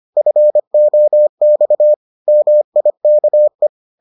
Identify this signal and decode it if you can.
Morse code, heard as a keyed tone, about 25 wpm, 600 Hz. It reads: FOX MIKE